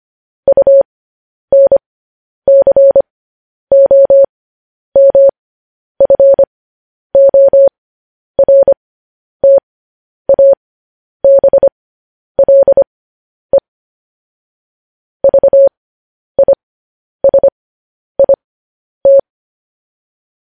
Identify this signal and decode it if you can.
Morse code, heard as a keyed tone, about 25 wpm, spaced as for 10 wpm, 550 Hz. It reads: UNCOMFORTABLE VISIT
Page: news-headlines-in-morse-code-at-10-wpm